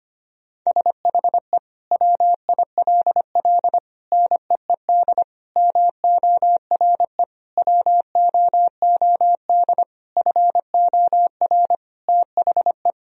Morse code, heard as a keyed tone, about 25 wpm, 700 Hz. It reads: SHE WILL NEED MORE WOOD FOR THE
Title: Sentences from Top 1000 Words - Review Part 2 - 25wpm